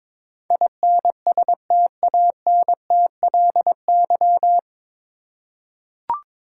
Morse code, heard as a keyed tone, about 22 wpm, 700 Hz. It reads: INSTANTLY
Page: Morse Code Ninja practice 22wpm